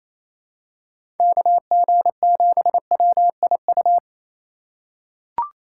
Morse code, heard as a keyed tone, about 28 wpm, 700 Hz. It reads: KG7WIU
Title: USA Call Signs - 6 Characters - 4x Character Spacing - 28wpm